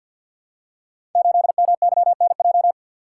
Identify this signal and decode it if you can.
Morse code, heard as a keyed tone, about 51 wpm, 700 Hz. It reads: 8MYNJ
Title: Code Groups - 5 Letter:Numbers - 50wpm